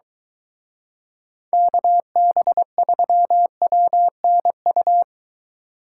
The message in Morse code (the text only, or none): KB3WNU